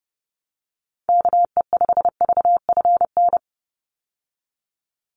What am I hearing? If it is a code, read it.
Morse code, heard as a keyed tone, about 30 wpm, 700 Hz. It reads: KE5VFN
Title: USA Call Signs 30wpm